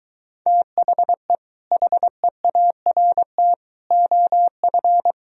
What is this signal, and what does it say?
Morse code, heard as a keyed tone, about 22 wpm, 700 Hz. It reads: THE HEART OF